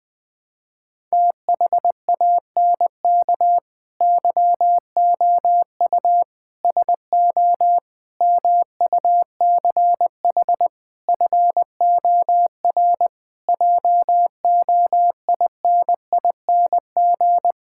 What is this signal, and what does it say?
Morse code, heard as a keyed tone, about 20 wpm, 700 Hz. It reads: THANK YOU SO MUCH FOR JOINING
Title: Sentences with 5-Word Phrases - 20wpm